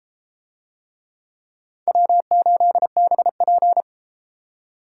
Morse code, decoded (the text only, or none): W8BP